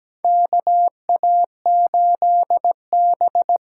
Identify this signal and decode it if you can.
Morse code, heard as a keyed tone, about 17 wpm, 700 Hz. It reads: KA8B
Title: USA Call Signs - 6 Characters - 8x Character Spacing - 17wpm